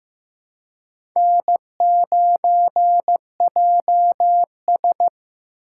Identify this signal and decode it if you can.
Morse code, heard as a keyed tone, about 15 wpm, 700 Hz. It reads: N9JS
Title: USA Call Signs - 4 Characters - 4x Character Spacing - 15wpm